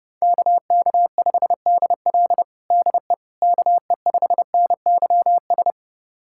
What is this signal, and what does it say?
Morse code, heard as a keyed tone, about 32 wpm, 700 Hz. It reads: KK5DL DE KE5NYS
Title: USA Callsign DE Callsign 30wpm